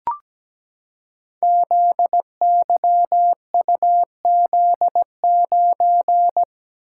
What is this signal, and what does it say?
Morse code, heard as a keyed tone, about 17 wpm, 700 Hz. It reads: ZYUZ9
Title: Code Groups - 5 Letter:Numbers - 17wpm